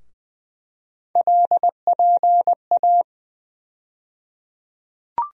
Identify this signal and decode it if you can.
Morse code, heard as a keyed tone, about 20 wpm, 700 Hz. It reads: LPA
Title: USA Call Sign Postfixes - 3 Letters 20wpm